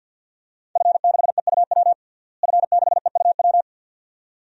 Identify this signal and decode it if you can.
Morse code, heard as a keyed tone, about 50 wpm, 700 Hz. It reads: F6EUW F6EUW
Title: DXCC-100 Call Signs - 5 Characters - Repeated 2x - 50wpm